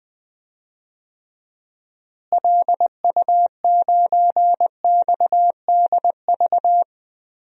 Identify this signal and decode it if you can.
Morse code, heard as a keyed tone, about 20 wpm, 700 Hz. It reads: LU9XDV
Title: DXCC-100 Call Signs - 6 Characters - Repeated 3x - 20wpm